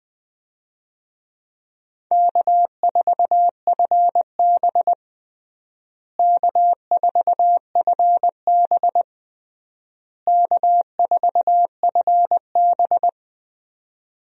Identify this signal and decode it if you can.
Morse code, heard as a keyed tone, about 20 wpm, 700 Hz. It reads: K4FB K4FB K4FB
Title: USA Call Signs - 4 Characters - Repeated 3x - 20wpm